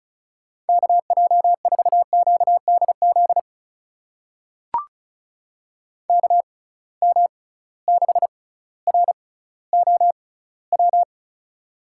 Morse code, heard as a keyed tone, about 36 wpm, spaced as for 18 wpm, 700 Hz. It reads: KJ4QDZ KM6ROW
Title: USA Call Signs - 6 Characters - 6x Character Spacing - 35wpm